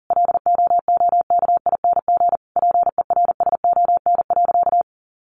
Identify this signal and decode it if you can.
Morse code, heard as a keyed tone, about 40 wpm, 700 Hz. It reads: LOOKING PERSON.
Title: Easy Phrases 40wpm